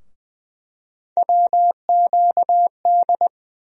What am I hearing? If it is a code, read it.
Morse code, heard as a keyed tone, about 21 wpm, 700 Hz. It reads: WQD
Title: USA Call Sign Postfixes - 3 Letters 20wpm